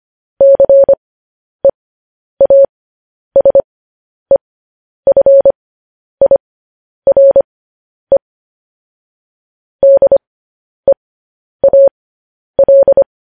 Morse code, heard as a keyed tone, about 25 wpm, spaced as for 10 wpm, 550 Hz. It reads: CEASEFIRE DEAL